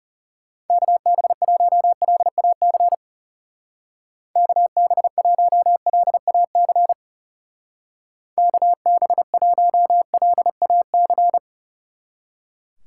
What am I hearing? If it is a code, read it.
Morse code, heard as a keyed tone, about 36 wpm, 700 Hz. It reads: KB1LAC KB1LAC KB1LAC
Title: Speed Racing - USA 2x3 Call Signs 30wpm